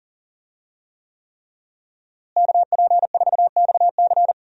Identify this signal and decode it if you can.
Morse code, heard as a keyed tone, about 40 wpm, 700 Hz. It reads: KP4XC
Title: DXCC-100 Call Signs - 5 Characters - Repeated 3x - 40wpm